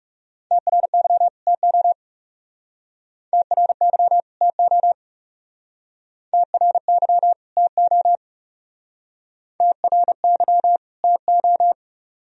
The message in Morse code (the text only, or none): TRY TO TRY TO TRY TO TRY TO